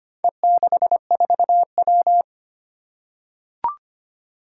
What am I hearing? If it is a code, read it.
Morse code, heard as a keyed tone, about 25 wpm, 700 Hz. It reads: E64W
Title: Four Challenging Letter-Number Combinations 25wpm